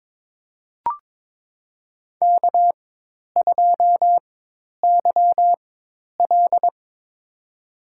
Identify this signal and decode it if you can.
Morse code, heard as a keyed tone, about 22 wpm, spaced as for 10 wpm, 700 Hz. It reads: K2YL
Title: USA Call Signs - 4 Characters - 4x Character Spacing - 22wpm